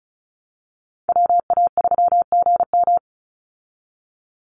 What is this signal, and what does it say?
Morse code, heard as a keyed tone, about 34 wpm, 700 Hz. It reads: WA3GM
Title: Call Signs Repeated 2x - 35wpm